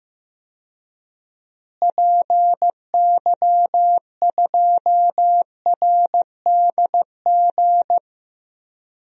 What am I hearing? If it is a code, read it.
Morse code, heard as a keyed tone, about 15 wpm, 700 Hz. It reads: PY2RDG
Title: DXCC-100 Call Signs - 6 Characters - Repeated 3x - 15wpm